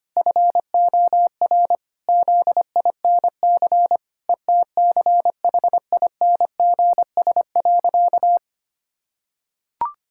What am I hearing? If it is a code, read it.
Morse code, heard as a keyed tone, about 25 wpm, 700 Hz. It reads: FOR ZINC ETCHINGS.
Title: Pangram Challenge v2 - 25wpm